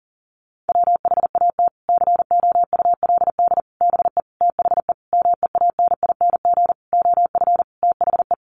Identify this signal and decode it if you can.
Morse code, heard as a keyed tone, about 40 wpm, 700 Hz. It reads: WHAT COULD BE THE MEANING OF THE
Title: Sentences from Top 900 Words 40wpm